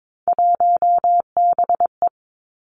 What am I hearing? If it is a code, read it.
Morse code, heard as a keyed tone, about 22 wpm, 700 Hz. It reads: 1BE
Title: Three Challenging Letter-Number Combinations 22wpm